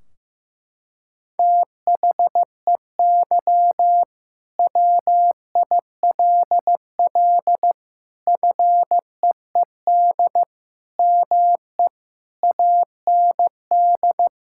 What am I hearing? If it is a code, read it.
Morse code, heard as a keyed tone, about 16 wpm, 700 Hz. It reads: THEY WILL FEED ME AND
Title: Sentences from Top 1600 Words 15wpm